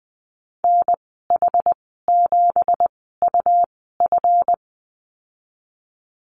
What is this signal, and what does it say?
Morse code, heard as a keyed tone, about 20 wpm, spaced as for 14 wpm, 700 Hz. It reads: NH7UF
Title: 179 Lesson 29g - Slash - Call Signs 20wpm